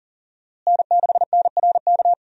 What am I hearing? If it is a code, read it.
Morse code, heard as a keyed tone, about 40 wpm, 700 Hz. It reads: N6NRK
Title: USA Call Signs - 5 Characters - Repeated 2x - 40wpm